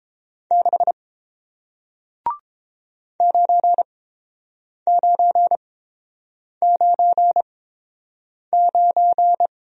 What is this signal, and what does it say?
Morse code, heard as a keyed tone, about 30 wpm, spaced as for 7 wpm, 700 Hz. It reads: 6 9999